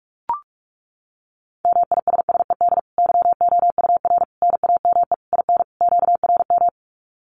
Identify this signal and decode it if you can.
Morse code, heard as a keyed tone, about 44 wpm, 700 Hz. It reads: MISSED YOUR NAME IN QRM
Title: Common QSO Phrases 45wpm